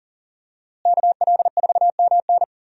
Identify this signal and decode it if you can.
Morse code, heard as a keyed tone, about 37 wpm, 700 Hz. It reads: KL4MN
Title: USA Call Signs - 5 Characters - Repeated 2x - 40wpm